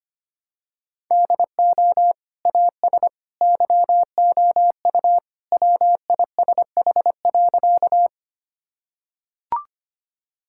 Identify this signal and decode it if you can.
Morse code, heard as a keyed tone, about 25 wpm, 700 Hz. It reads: DO AS YOU WISH.
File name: practice / 25wpm / Sentences with 2-Word Phrases - 25wpm